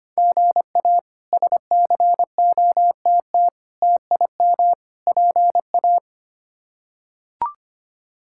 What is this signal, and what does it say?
Morse code, heard as a keyed tone, about 25 wpm, 700 Hz. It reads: GA SCOTT TIM PA
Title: K1USN SST QSO Practice - 25wpm